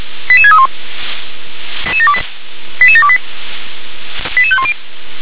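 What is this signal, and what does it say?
Five-tone selective call (ZVEI), heard as quick runs of five tones, tones of about 70 ms. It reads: 88631 88728 88510
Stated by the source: ZVEI five-tone selective call 88730 received on an SDR.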